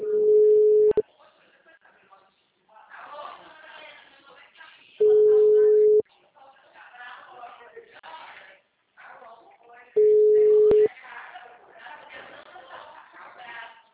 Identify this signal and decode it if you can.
Ringback tone (Europe, 425 Hz), with speech.
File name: out- -2007- - - .47536